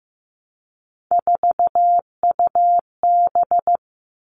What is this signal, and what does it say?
Morse code, heard as a keyed tone, about 15 wpm, 700 Hz. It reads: 4UB
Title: All Cut Numbers - 3-Digit 15wpm